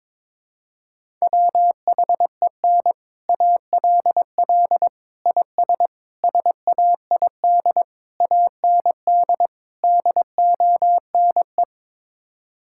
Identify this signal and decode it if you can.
Morse code, heard as a keyed tone, about 22 wpm, 700 Hz. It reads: WHEN ALL IS SAID AND DONE
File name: practice / 22wpm / Sets of 6 Words - 22wpm